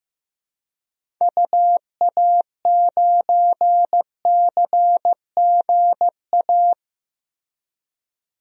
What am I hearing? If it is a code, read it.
Morse code, heard as a keyed tone, about 15 wpm, 700 Hz. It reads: UA9CGA
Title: DXCC-100 Call Signs - 6 Characters - Repeated 3x - 15wpm